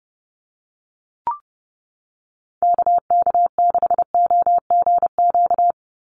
Morse code, heard as a keyed tone, about 30 wpm, 700 Hz. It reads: KK6OGQ
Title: USA Call Signs 30wpm